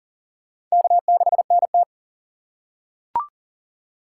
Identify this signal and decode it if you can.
Morse code, heard as a keyed tone, about 40 wpm, 700 Hz. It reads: K6NT
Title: USA Call Signs - 3x Character Spacing - 40wpm